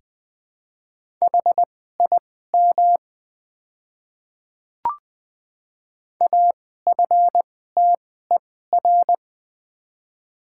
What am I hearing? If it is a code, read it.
Morse code, heard as a keyed tone, about 20 wpm, spaced as for 15 wpm, 700 Hz. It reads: HIM AFTER
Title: 2x Character Spacing - Top 200 Words - 20wpm